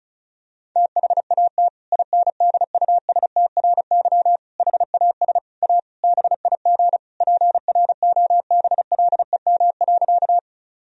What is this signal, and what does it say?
Morse code, heard as a keyed tone, about 36 wpm, 700 Hz. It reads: THAT INDUSTRY HAS A BIG PROBLEM.
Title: Sentences from Top 1000 Words - Review Part 4 - 35wpm